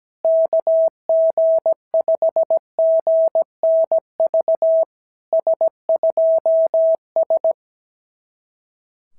Morse code, heard as a keyed tone, about 17 wpm, 650 Hz. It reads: KG5GNV S2S
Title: SOTA QSO Practice 17wpm